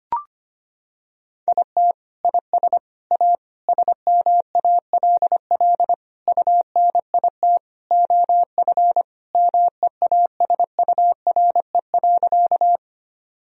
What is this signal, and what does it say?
Morse code, heard as a keyed tone, about 25 wpm, 700 Hz. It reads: IT IS A SMALL UNIT OF MEASURE.
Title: Sentences from Top 1000 Words - Review Part 2 - 25wpm